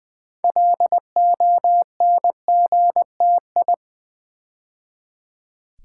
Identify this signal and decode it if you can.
Morse code, heard as a keyed tone, about 20 wpm, 700 Hz. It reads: LONGTI
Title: Build A Word - 2 Letters At A Time - 8-Letter Words - 20wpm